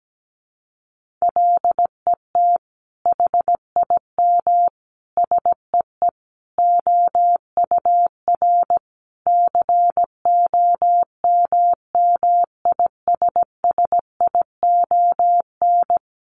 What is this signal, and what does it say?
Morse code, heard as a keyed tone, about 17 wpm, 700 Hz. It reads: LET HIM SEE OUR COMMISSION
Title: Sentences from Top 1400 Words 17wpm